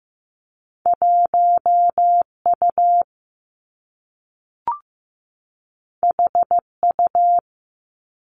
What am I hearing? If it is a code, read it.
Morse code, heard as a keyed tone, about 15 wpm, 700 Hz. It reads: 1 U H U
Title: Two Challenging Letter-Number Combinations 15wpm